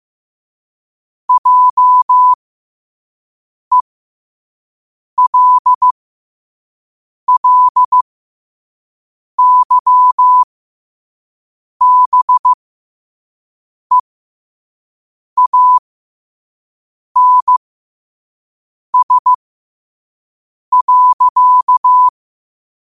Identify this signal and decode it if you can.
Morse code, heard as a keyed tone, about 15 wpm, spaced as for 5 wpm, 1000 Hz. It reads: JELLYBEANS.
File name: qotdfiles / 5wpm